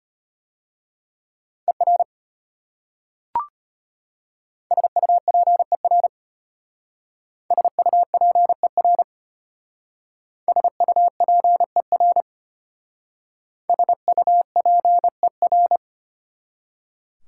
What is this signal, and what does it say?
Morse code, heard as a keyed tone, about 32 wpm, 700 Hz. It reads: ER SUPER SUPER SUPER SUPER